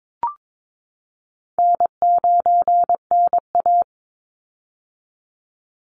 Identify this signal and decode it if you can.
Morse code, heard as a keyed tone, about 22 wpm, 700 Hz. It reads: N9NA
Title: Easy Call Signs - 22wpm